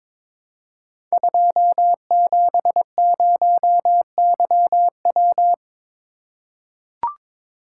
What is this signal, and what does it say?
Morse code, heard as a keyed tone, about 23 wpm, 700 Hz. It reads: 270YW
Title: Code Groups - 5 Letter:Numbers - 22wpm